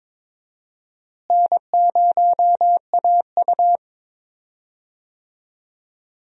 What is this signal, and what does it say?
Morse code, heard as a keyed tone, about 22 wpm, 700 Hz. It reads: N0AU
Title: USA Call Signs - 4 Characters - Repeated 3x - 22wpm